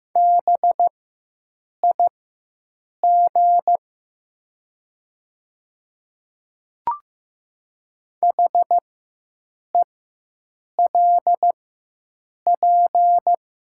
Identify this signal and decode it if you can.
Morse code, heard as a keyed tone, about 15 wpm, spaced as for 7 wpm, 700 Hz. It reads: BIG HELP
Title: 4x Character Spacing - Top 300 Words - 15wpm